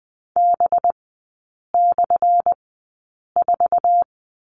Morse code, heard as a keyed tone, about 20 wpm, spaced as for 8 wpm, 700 Hz. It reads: B/4
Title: 184 Lesson 30e - Letter b - Three Characters 20wpm